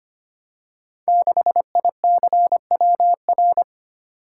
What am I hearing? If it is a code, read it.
Morse code, heard as a keyed tone, about 25 wpm, 700 Hz. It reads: 6ICWR
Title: Code Groups - 5 Letter:Numbers - 25wpm